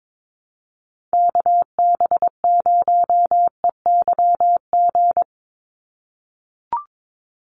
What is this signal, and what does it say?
Morse code, heard as a keyed tone, about 22 wpm, 700 Hz. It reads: KB0EYG
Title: USA Call Signs 22wpm